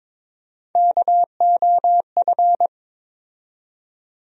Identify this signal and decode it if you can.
Morse code, heard as a keyed tone, about 22 wpm, 700 Hz. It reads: KOF
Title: USA Call Sign Postfixes - 3 Letters 22wpm